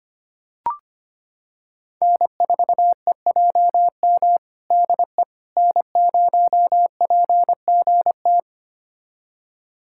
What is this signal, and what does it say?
Morse code, heard as a keyed tone, about 25 wpm, 700 Hz. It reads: N4EJM DE N0PGT
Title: USA Callsign DE Callsign - 5 Characters - 25wpm